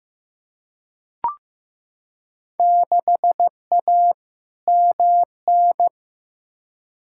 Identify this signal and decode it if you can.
Morse code, heard as a keyed tone, about 15 wpm, 700 Hz. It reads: 6A MN
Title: Field Day Exchanges Encore - 15wpm